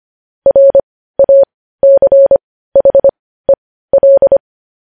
Morse code, heard as a keyed tone, about 25 wpm, spaced as for 15 wpm, 550 Hz. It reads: RACHEL
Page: News Headlines in Morse Code at 15 WPM